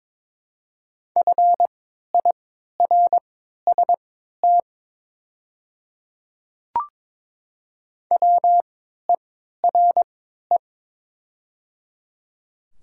Morse code, heard as a keyed tone, about 22 wpm, spaced as for 12 wpm, 700 Hz. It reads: FIRST WERE
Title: 3x Character Spacing - Top 100 Words - 22wpm